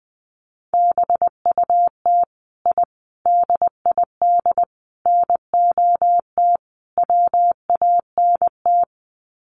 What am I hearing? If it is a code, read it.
Morse code, heard as a keyed tone, about 20 wpm, 700 Hz. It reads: BUT I DID NOT WANT
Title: Sets of 5 Words Encore 20wpm